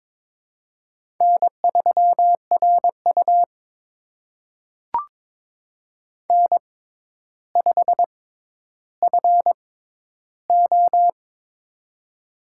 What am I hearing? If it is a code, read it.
Morse code, heard as a keyed tone, about 22 wpm, spaced as for 11 wpm, 700 Hz. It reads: N3RU N5FO
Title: USA Call Signs - 4 Characters - 6x Character Spacing - 22wpm